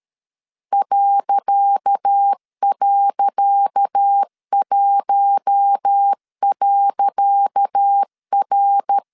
Morse code, heard as a keyed tone, about 13 wpm, 800 Hz. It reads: ..1.R
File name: LEZIONI / 10 SIGNS / 10C DOT 12wpm 2451